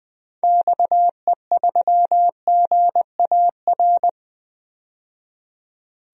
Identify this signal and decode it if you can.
Morse code, heard as a keyed tone, about 20 wpm, 700 Hz. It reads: XE3GAR
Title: DXCC-100 Call Signs - 6 Characters - 20wpm